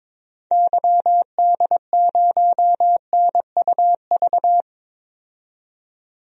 Morse code, heard as a keyed tone, about 22 wpm, 700 Hz. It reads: YD0NUV
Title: DXCC-100 Call Signs - 6 Characters - 22wpm